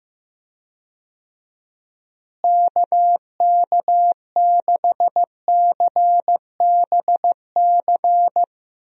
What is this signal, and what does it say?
Morse code, heard as a keyed tone, about 15 wpm, 700 Hz. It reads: KK6CBC
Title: USA Call Signs - 6 Characters - Repeated 3x - 15wpm